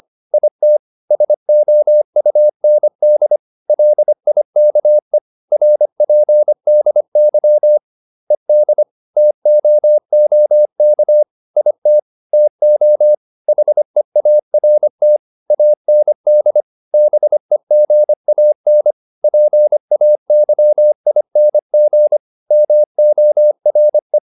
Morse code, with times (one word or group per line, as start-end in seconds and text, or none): 0.3-0.8 IT
1.1-3.4 SOUND
3.7-5.2 LIKE
5.5-7.8 RPDY
8.3-8.8 ED
9.2-11.2 TOOK
11.6-12.0 IT
12.3-13.1 TO
13.5-15.2 HEART
15.5-16.6 AND
16.9-18.9 BEGAN
19.2-22.2 PAYING
22.5-24.2 MORE